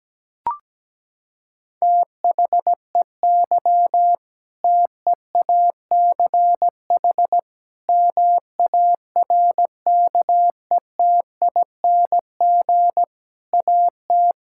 Morse code, heard as a keyed tone, about 17 wpm, 700 Hz. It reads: THEY TEACH MARKETING AT
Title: Sentences from Top 1000 Words - Review Part 5 - 17wpm